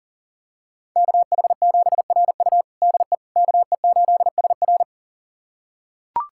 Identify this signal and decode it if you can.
Morse code, heard as a keyed tone, about 40 wpm, 700 Hz. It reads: KH7RU DE KE8SR
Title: USA Callsign DE Callsign - 5 Characters - 40wpm